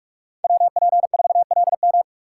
Morse code, heard as a keyed tone, about 47 wpm, 700 Hz. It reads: WP4LM
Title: USA Call Signs - 5 Characters - 6x Character Spacing - 45wpm